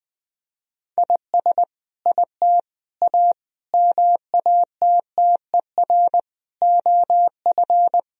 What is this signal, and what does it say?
Morse code, heard as a keyed tone, about 20 wpm, 700 Hz. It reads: IS IT A MATTER OF
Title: Sentences from Top 1000 Words - Review Part 3 - 20wpm